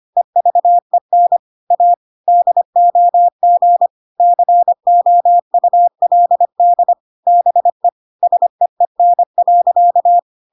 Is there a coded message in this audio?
Yes: Morse code from 0.2 to 10.2 s, heard as a keyed tone, about 25 wpm, 700 Hz. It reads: EVEN A DOG COULD BE SEEN.